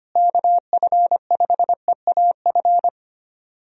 Morse code, heard as a keyed tone, about 25 wpm, 700 Hz. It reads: KF5EAF
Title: USA Call Signs - 6 Characters - Repeated 2x - 25wpm